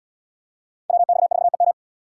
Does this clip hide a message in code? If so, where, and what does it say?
Yes: Morse code from 0.9 to 1.7 s, heard as a keyed tone, about 75 wpm, 700 Hz. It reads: WX4EM